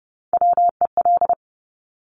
Morse code, heard as a keyed tone, about 30 wpm, 700 Hz. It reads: WEL
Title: Top Three Letter Combinations in Words 30wpm